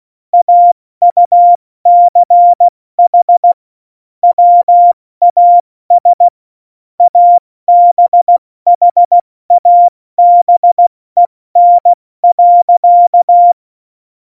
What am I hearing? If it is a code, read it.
Morse code, heard as a keyed tone, about 16 wpm, 700 Hz. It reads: AUCH WAS ABHABEN.
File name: Morsekurs / Witze / Witz 08. Dezember 2020, 70 F80